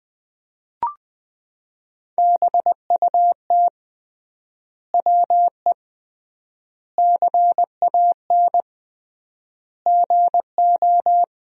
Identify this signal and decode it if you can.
Morse code, heard as a keyed tone, about 20 wpm, 700 Hz. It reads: BUT WE CAN GO
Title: 3x Word-Spacing - Sentences from Top 100 Words - 20wpm